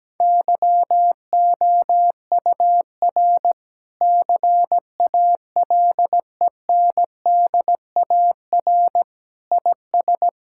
Morse code, heard as a keyed tone, about 17 wpm, 700 Hz. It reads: YOUR CALENDAR IS
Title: Sentences from Top 1600 Words 17wpm